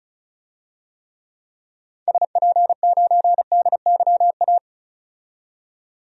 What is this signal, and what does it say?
Morse code, heard as a keyed tone, about 36 wpm, 700 Hz. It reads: SP9DYA
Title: DXCC-100 Call Signs - 6 Characters - 35wpm